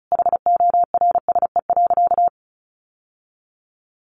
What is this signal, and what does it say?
Morse code, heard as a keyed tone, about 34 wpm, 700 Hz. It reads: HORSE.
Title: Sentences from Top 1100 Words 35wpm